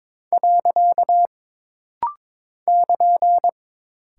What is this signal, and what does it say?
Morse code, heard as a keyed tone, about 22 wpm, 700 Hz. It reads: . <KN>
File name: practice / 22wpm / Single Punctuation-Prosign - Rapid-Fire - 22wpm